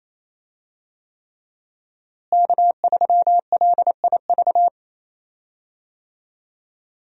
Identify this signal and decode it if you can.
Morse code, heard as a keyed tone, about 28 wpm, 700 Hz. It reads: K3LIV
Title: USA Call Signs - 5 Characters - Repeated 3x - 28wpm